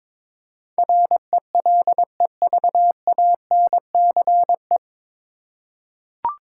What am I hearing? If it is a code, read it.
Morse code, heard as a keyed tone, about 22 wpm, 700 Hz. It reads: RELEVANCE
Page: Morse Code Ninja practice 22wpm